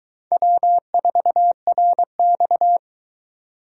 Morse code, heard as a keyed tone, about 22 wpm, 700 Hz. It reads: W4RX